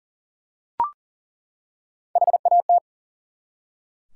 Morse code, heard as a keyed tone, about 42 wpm, 700 Hz. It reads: HAT
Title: Three Letter Words - 40wpm